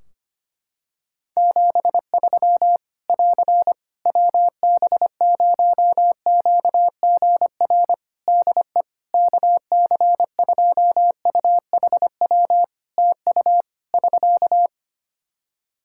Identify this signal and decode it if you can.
Morse code, heard as a keyed tone, about 25 wpm, 700 Hz. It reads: 73 <AR> WB0QGR DE KC2UHW TU <SK>
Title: Practice QSO - Exchange 4 - Ending - 25wpm